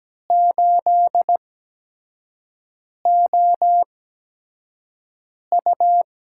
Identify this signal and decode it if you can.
Morse code, heard as a keyed tone, about 17 wpm, spaced as for 5 wpm, 700 Hz. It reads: 8OU
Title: USA Call Signs - 4 Characters - 8x Character Spacing - 17wpm